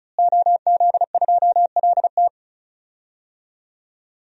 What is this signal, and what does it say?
Morse code, heard as a keyed tone, about 34 wpm, 700 Hz. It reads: OZ2LT
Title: DXCC-100 Call Signs - 5 Characters - Repeated 2x - 35wpm